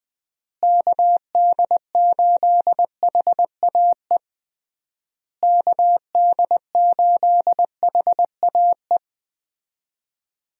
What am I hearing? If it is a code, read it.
Morse code, heard as a keyed tone, about 21 wpm, 700 Hz. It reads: KD8HAE KD8HAE
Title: USA Call Signs - 6 Characters - Repeated 3x - 20wpm